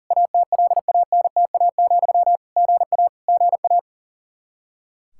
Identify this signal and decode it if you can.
Morse code, heard as a keyed tone, about 40 wpm, 700 Hz. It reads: ATLANTA, GA GA
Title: K1USN SST QSO Practice 40wpm